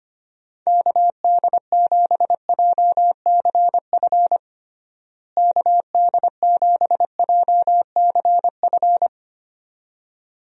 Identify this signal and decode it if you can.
Morse code, heard as a keyed tone, about 25 wpm, 700 Hz. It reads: KD7JCF KD7JCF
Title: USA Call Signs - 6 Characters - Repeated 2x - 25wpm